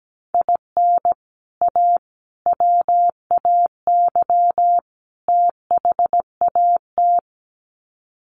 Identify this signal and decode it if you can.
Morse code, heard as a keyed tone, about 17 wpm, 700 Hz. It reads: IN A WAY THAT
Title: Sets of 4 Words 17wpm